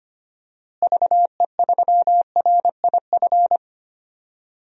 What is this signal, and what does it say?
Morse code, heard as a keyed tone, about 25 wpm, 700 Hz. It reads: VE3RIF
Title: DXCC-100 Call Signs - 6 Characters - Repeated 3x - 25wpm